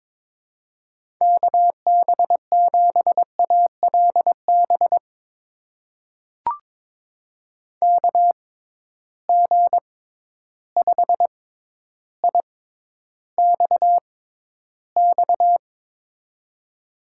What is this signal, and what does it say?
Morse code, heard as a keyed tone, about 22 wpm, spaced as for 11 wpm, 700 Hz. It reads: KB7ALB KG5IXX
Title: USA Call Signs - 6x Character Spacing - 22wpm